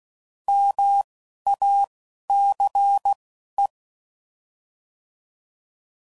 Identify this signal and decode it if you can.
Morse code, heard as a keyed tone, about 16 wpm, spaced as for 11 wpm, 800 Hz. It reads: MACE